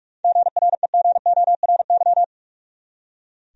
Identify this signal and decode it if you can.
Morse code, heard as a keyed tone, about 46 wpm, 700 Hz. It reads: GREGORY
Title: USA Common Names - 45wpm